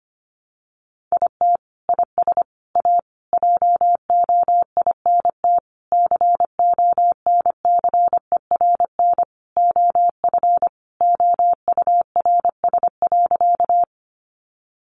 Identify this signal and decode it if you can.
Morse code, heard as a keyed tone, about 25 wpm, 700 Hz. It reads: IT IS A JOINT CONCERN OF OURS.
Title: Sentences from Top 1300 Words 25wpm